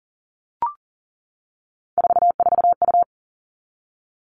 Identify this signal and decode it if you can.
Morse code, heard as a keyed tone, about 40 wpm, 700 Hz. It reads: 44U
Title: All Cut Numbers - 3-Digit 40wpm